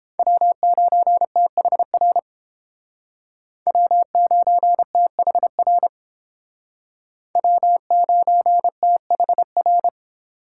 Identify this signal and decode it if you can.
Morse code, heard as a keyed tone, about 30 wpm, 700 Hz. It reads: W9THR W9THR W9THR